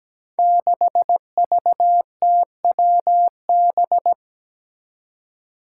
Morse code, heard as a keyed tone, about 17 wpm, 700 Hz. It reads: 6VTWB
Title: Five Challenging Letter-Number Combinations 17wpm